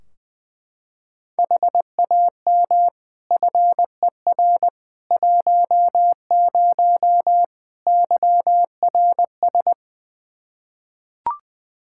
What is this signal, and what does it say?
Morse code, heard as a keyed tone, about 20 wpm, 700 Hz. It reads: HAM FER 10 YRS
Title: Practice QSO - Exchange 3 - More Info - 20wpm